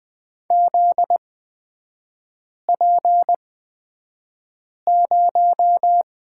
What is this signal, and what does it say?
Morse code, heard as a keyed tone, about 20 wpm, spaced as for 5 wpm, 700 Hz. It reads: ZP0
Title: 243 Lesson 39e - Letter z - Three Characters 20wpm